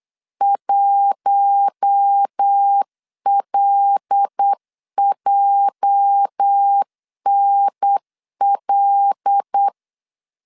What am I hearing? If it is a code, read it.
Morse code, heard as a keyed tone, about 8 wpm, 800 Hz. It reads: 1LJNL
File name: LEZIONI / 8 YJ45 / 8 YJ 8wpm 3313